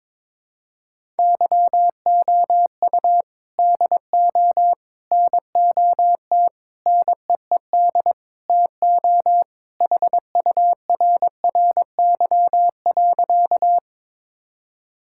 Morse code, heard as a keyed tone, about 22 wpm, 700 Hz. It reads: YOU DO NOT NEED TO HURRY.
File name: practice / 22wpm / Sentences with 4-Word Phrases - 22wpm